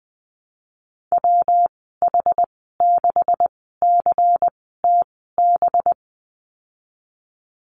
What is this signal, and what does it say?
Morse code, heard as a keyed tone, about 20 wpm, spaced as for 15 wpm, 700 Hz. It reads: WH6CTB
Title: 186 Lesson 30g - Letter b - Call Signs 20wpm